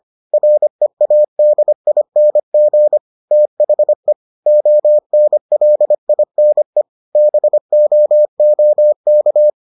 Morse code, heard as a keyed tone, about 25 wpm, 600 Hz. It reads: READING THE ONLINE BOOK